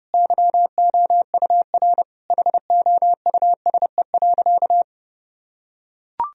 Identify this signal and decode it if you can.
Morse code, heard as a keyed tone, about 30 wpm, 700 Hz. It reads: YOUR HOUSE.
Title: Sentences from Top 1000 Words - Review Part 2 - 30wpm